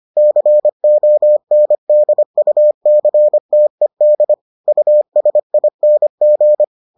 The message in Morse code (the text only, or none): CONDUCTED USING